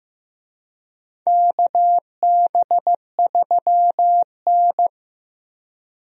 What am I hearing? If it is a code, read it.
Morse code, heard as a keyed tone, about 15 wpm, 700 Hz. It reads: KB3N
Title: USA Call Signs - 4 Characters - 15wpm